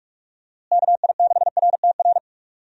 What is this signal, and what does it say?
Morse code, heard as a keyed tone, about 45 wpm, 700 Hz. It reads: KI6RTR
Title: USA Call Signs - 6 Characters - Repeated 2x - 45wpm